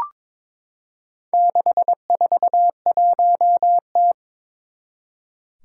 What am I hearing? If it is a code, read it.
Morse code, heard as a keyed tone, about 22 wpm, 700 Hz. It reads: 641T
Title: Four Challenging Letter-Number Combinations 22wpm